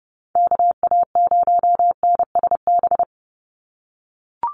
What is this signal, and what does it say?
Morse code, heard as a keyed tone, about 30 wpm, 700 Hz. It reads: KA0NSB
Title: USA Call Signs 30wpm